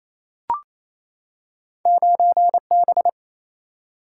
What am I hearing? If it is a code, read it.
Morse code, heard as a keyed tone, about 28 wpm, 700 Hz. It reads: 9B
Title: Single-Letter Single-Number 28wpm